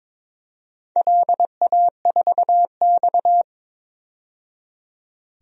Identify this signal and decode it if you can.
Morse code, heard as a keyed tone, about 22 wpm, 700 Hz. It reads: LA4X
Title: DXCC-100 Call Signs - 4 Characters - 22wpm